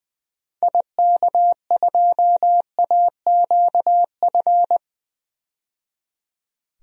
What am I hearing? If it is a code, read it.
Morse code, heard as a keyed tone, about 20 wpm, 700 Hz. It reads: IK2AQF
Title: DXCC-100 Call Signs - 6 Characters - 20wpm